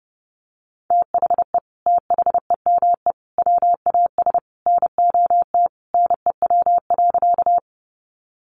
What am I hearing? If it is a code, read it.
Morse code, heard as a keyed tone, about 30 wpm, 700 Hz. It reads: THE THEME WAS NOT NEW.
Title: Sentences from Top 1100 Words 30wpm